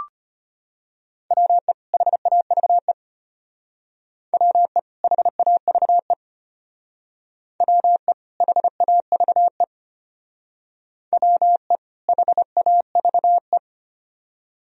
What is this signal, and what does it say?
Morse code, heard as a keyed tone, about 32 wpm, 700 Hz. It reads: WE HAVE WE HAVE WE HAVE WE HAVE